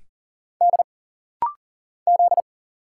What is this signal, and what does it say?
Morse code, heard as a keyed tone, about 40 wpm, 700 Hz. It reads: D Z
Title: Single Letters - Rapid-Fire 40wpm